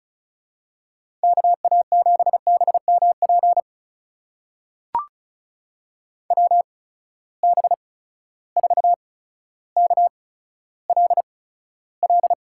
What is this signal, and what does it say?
Morse code, heard as a keyed tone, about 36 wpm, spaced as for 15 wpm, 700 Hz. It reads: KA7BMP WB4KLL
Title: USA Call Signs - 6 Characters - 8x Character Spacing - 35wpm